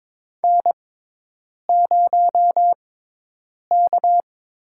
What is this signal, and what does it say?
Morse code, heard as a keyed tone, about 23 wpm, spaced as for 8 wpm, 700 Hz. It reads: N0K
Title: USA Call Signs - 4 Characters - 6x Character Spacing - 22wpm